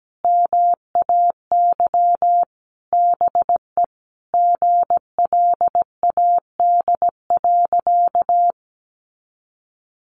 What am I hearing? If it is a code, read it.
Morse code, heard as a keyed tone, about 17 wpm, 700 Hz. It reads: MAY BE GLAD.
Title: Sentences from Top 800 Words 17wpm